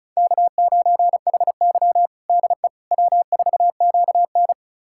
Morse code, heard as a keyed tone, about 36 wpm, 700 Hz. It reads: K9HY DE W4QN
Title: USA Callsign DE Callsign - 4 Characters - 35wpm